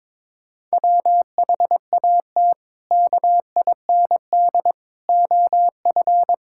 Morse code, heard as a keyed tone, about 22 wpm, 700 Hz. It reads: WHAT KIND OF
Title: Sentences from Top 1000 Words - Review Part 5 - 22wpm